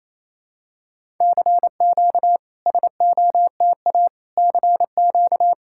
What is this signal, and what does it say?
Morse code, heard as a keyed tone, about 28 wpm, 700 Hz. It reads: CQ SOTA CQ
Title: SOTA QSO Practice 28wpm